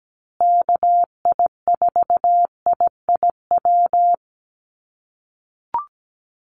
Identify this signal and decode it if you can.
Morse code, heard as a keyed tone, about 17 wpm, 700 Hz. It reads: KI4IIW
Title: USA Call Signs Repeated 3x 17wpm